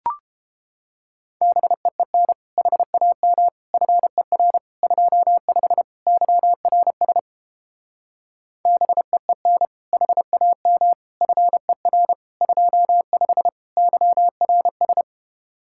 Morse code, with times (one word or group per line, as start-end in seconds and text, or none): 1.4-2.3 BEEN
2.6-3.5 HAM
3.7-4.6 FER
4.8-5.8 25
6.1-7.2 YRS
8.7-9.7 BEEN
9.9-10.9 HAM
11.2-12.1 FER
12.4-13.5 25
13.8-15.0 YRS